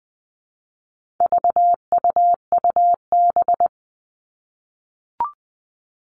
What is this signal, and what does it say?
Morse code, heard as a keyed tone, about 20 wpm, 700 Hz. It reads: VUUB
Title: All Cut Numbers - 4-Digit 20wpm